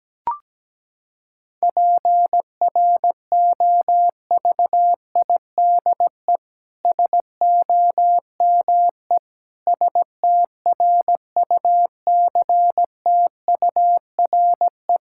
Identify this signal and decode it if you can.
Morse code, heard as a keyed tone, about 17 wpm, 700 Hz. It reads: PROVIDE SOME STRUCTURE
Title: Sentences from Top 1000 Words - Review Part 6 - 17wpm